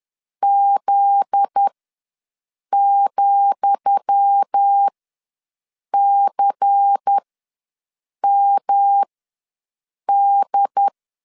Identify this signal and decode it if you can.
Morse code, heard as a keyed tone, about 10 wpm, spaced as for 6 wpm, 800 Hz. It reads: Z,CMD
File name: LEZIONI / 10 SIGNS / 10D COMA ws 10wpm 4314